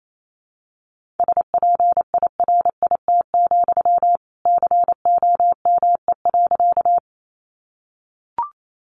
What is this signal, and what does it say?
Morse code, heard as a keyed tone, about 28 wpm, 700 Hz. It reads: SPIRIT, COME.
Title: Sentences from Top 1400 Words 28wpm